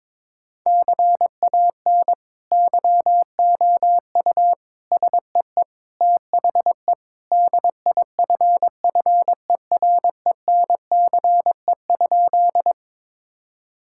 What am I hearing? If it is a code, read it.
Morse code, heard as a keyed tone, about 22 wpm, 700 Hz. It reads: CAN YOU SEE THE DIFFERENCE?
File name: practice / 22wpm / Sentences with 2-Word Phrases - 22wpm